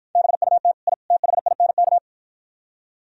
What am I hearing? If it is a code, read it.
Morse code, heard as a keyed tone, about 52 wpm, 700 Hz. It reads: BUT I THINK